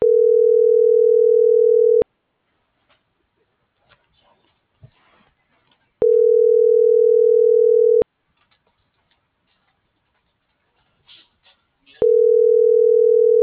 Ringback tone (North America), 440 + 480 Hz.